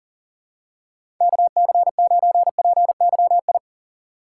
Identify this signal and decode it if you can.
Morse code, heard as a keyed tone, about 40 wpm, 700 Hz. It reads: KC9PYI
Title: USA Call Signs - 6 Characters - 2x Character Spacing - 40wpm